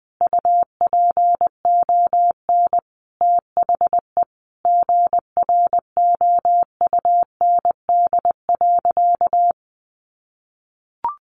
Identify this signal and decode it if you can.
Morse code, heard as a keyed tone, about 20 wpm, 700 Hz. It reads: UPON THE GROUND.
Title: Sentences from Top 1100 Words 20wpm